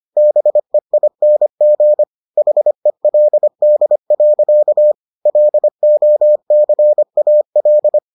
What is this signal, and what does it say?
Morse code, heard as a keyed tone, about 25 wpm, 600 Hz. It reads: BEING HELD. LOCAL